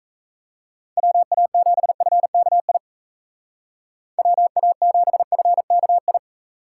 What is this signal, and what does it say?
Morse code, heard as a keyed tone, about 34 wpm, 700 Hz. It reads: WA7FKI WA7FKI
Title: Speed Racing - USA 2x3 Call Signs 28wpm